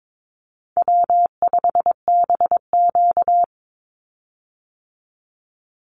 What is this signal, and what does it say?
Morse code, heard as a keyed tone, about 22 wpm, 700 Hz. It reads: W5BQ
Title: Easy Call Signs Repeated 2x - 22wpm